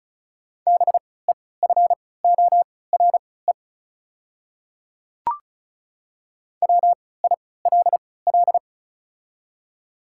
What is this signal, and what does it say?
Morse code, heard as a keyed tone, about 36 wpm, spaced as for 20 wpm, 700 Hz. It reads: BEFORE WILL
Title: 3x Character Spacing - Top 200 Words - 35wpm